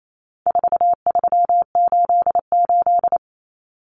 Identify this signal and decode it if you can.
Morse code, heard as a keyed tone, about 28 wpm, 700 Hz. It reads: 4388
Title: Common Cut Numbers - 4-Digit 28wpm